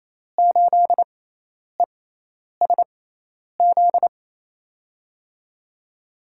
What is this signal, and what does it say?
Morse code, heard as a keyed tone, about 28 wpm, spaced as for 10 wpm, 700 Hz. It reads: 8ESZ
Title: USA Call Signs - 6x Character Spacing - 28wpm